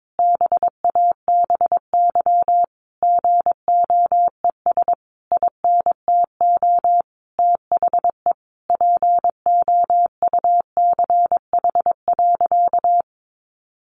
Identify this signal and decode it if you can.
Morse code, heard as a keyed tone, about 22 wpm, 700 Hz. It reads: BABY GOES INTO THE POUCH.
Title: Easy Phrases 22wpm